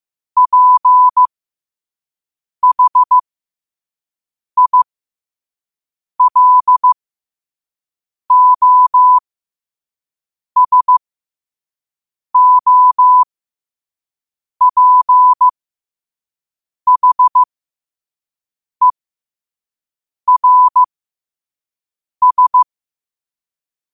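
Morse code, heard as a keyed tone, about 15 wpm, spaced as for 5 wpm, 1000 Hz. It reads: PHILOSOPHERS
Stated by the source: Morse code quote of the day at 5 WPM.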